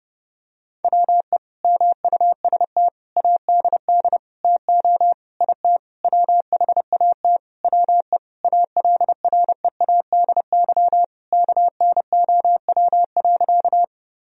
Morse code, heard as a keyed tone, about 29 wpm, 700 Hz. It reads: WE MUST ADD TO IT WHAT WE ALREADY KNOW.
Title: Sentences from Top 1000 Words - Review Part 1 - 30wpm